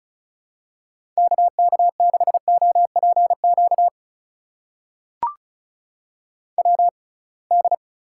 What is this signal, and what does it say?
Morse code, heard as a keyed tone, about 36 wpm, spaced as for 27 wpm, 700 Hz. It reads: KK6OPQ WD
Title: USA Call Signs - 6x Character Spacing - 35wpm